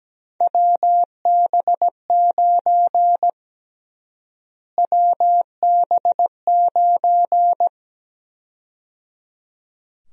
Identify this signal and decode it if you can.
Morse code, heard as a keyed tone, about 17 wpm, 700 Hz. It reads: WB9 WB9
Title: USA Call Sign Prefixes - Repeated 2x - 17wpm